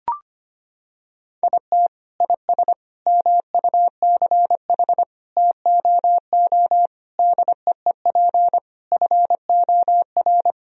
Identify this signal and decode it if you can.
Morse code, heard as a keyed tone, about 25 wpm, 700 Hz. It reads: IT IS MUCH TOO DEEP FOR
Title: Sentences from Top 1000 Words - Review Part 3 - 25wpm